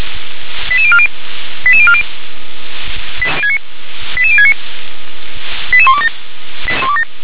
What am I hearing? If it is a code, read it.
Five-tone selective call (ZVEI), heard as quick runs of five tones, tones of about 70 ms. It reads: 88040 88040 88069 88127
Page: /357/SDR/ZVEI/88168/2022.10.28/